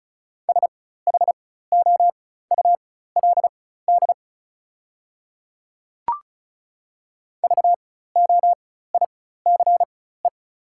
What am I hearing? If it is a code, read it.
Morse code, heard as a keyed tone, about 35 wpm, spaced as for 16 wpm, 690 Hz. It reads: SHOULD VOICE
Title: 4x Character Spacing - Top 500 Words - 35wpm